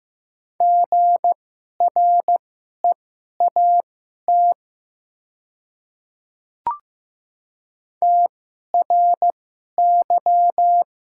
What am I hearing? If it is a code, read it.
Morse code, heard as a keyed tone, about 15 wpm, spaced as for 11 wpm, 700 Hz. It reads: GREAT TRY
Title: 2x Character Spacing - Top 300 Words - 15wpm